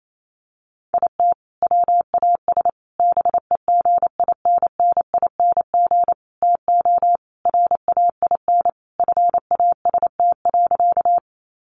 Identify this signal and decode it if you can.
Morse code, heard as a keyed tone, about 28 wpm, 700 Hz. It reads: IT WAS BEGINNING TO RAIN FAST.
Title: Sentences from Top 800 Words 28wpm